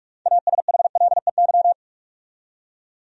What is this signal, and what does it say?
Morse code, heard as a keyed tone, about 48 wpm, 700 Hz. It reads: ASHLEY
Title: USA Common Names - 45wpm